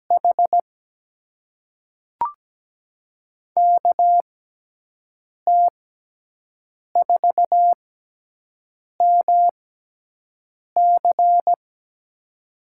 Morse code, heard as a keyed tone, about 17 wpm, spaced as for 6 wpm, 700 Hz. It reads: H KT4MC
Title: USA Call Signs - 5 Characters - 6x Character Spacing - 17wpm